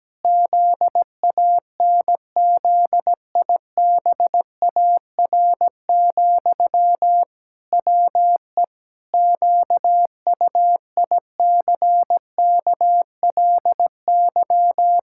Morse code, heard as a keyed tone, about 17 wpm, 700 Hz. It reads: ZANZIBAR, WE QUICKLY
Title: Pangram Challenge v2 - 17wpm